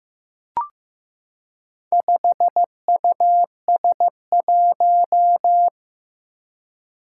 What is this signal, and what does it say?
Morse code, heard as a keyed tone, about 15 wpm, 700 Hz. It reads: 5US1
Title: Four Challenging Letter-Number Combinations 15wpm